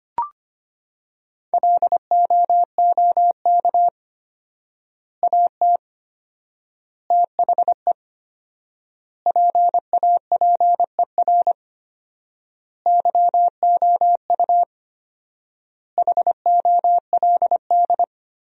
Morse code, heard as a keyed tone, about 25 wpm, 700 Hz. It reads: LOOK AT THE PAPER YOU HOLD
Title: 4x Word-Spacing - Sentences from Top 400 Words - 25wpm